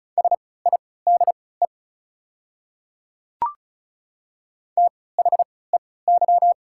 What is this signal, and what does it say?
Morse code, heard as a keyed tone, about 34 wpm, spaced as for 20 wpm, 690 Hz. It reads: SIDE THEY
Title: 3x Character Spacing - Top 100 Words - 35wpm